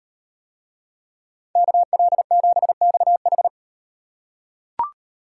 Morse code, heard as a keyed tone, about 38 wpm, 700 Hz. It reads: KL7XH